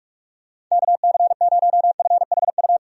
Morse code, heard as a keyed tone, about 41 wpm, 700 Hz. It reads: KC0FHU
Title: USA Call Signs - 6 Characters - 45wpm